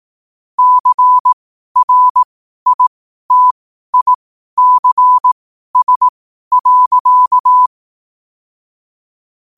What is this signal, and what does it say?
Morse code, heard as a keyed tone, about 18 wpm, spaced as for 12 wpm, 1000 Hz. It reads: CRITICS.
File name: qotdfiles / 12wpm